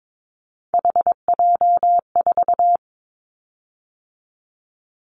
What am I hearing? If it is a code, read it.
Morse code, heard as a keyed tone, about 22 wpm, 700 Hz. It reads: HJ4
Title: Callsign Prefixes for Top 100 DXCC - 22wpm